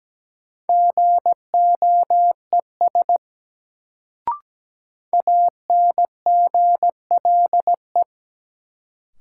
Morse code, heard as a keyed tone, about 17 wpm, 700 Hz. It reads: GOES ANGLE
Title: Sentences from Top 1500 Words - New Word Drill Alt - 17wpm